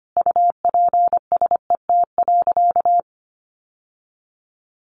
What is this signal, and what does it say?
Morse code, heard as a keyed tone, about 25 wpm, 700 Hz. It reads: UPSET.
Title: Sentences from Top 1200 Words 25wpm